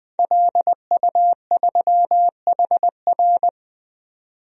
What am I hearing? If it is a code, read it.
Morse code, heard as a keyed tone, about 20 wpm, 700 Hz. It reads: LU3HR
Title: DXCC-100 Call Signs - 5 Characters - 20wpm